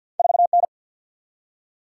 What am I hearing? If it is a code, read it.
Morse code, heard as a keyed tone, about 50 wpm, 700 Hz. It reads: 4 N
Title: Single-Letter Single-Number 50wpm